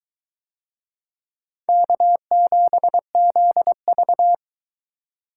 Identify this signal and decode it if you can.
Morse code, heard as a keyed tone, about 21 wpm, 700 Hz. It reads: K7ZV